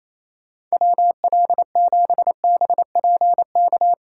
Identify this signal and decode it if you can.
Morse code, heard as a keyed tone, about 35 wpm, 700 Hz. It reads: WL7BPK
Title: Speed Racing - USA 2x3 Call Signs 28wpm